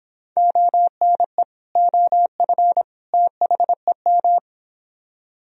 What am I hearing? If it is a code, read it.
Morse code, heard as a keyed tone, about 26 wpm, 700 Hz. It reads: ONE OF THEM